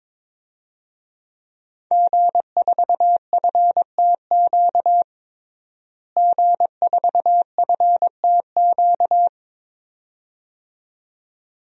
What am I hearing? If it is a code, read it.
Morse code, heard as a keyed tone, about 22 wpm, 700 Hz. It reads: G4FTQ G4FTQ
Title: DXCC-100 Call Signs - 5 Characters - Repeated 2x - 22wpm